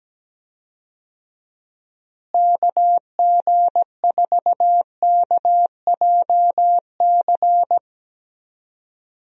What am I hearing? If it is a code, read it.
Morse code, heard as a keyed tone, about 17 wpm, 700 Hz. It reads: KG4KJC
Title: USA Call Signs - 6 Characters - Repeated 2x - 17wpm